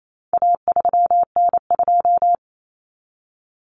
Morse code, heard as a keyed tone, about 28 wpm, 700 Hz. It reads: A3N2
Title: Common Cut Numbers - 4-Digit 28wpm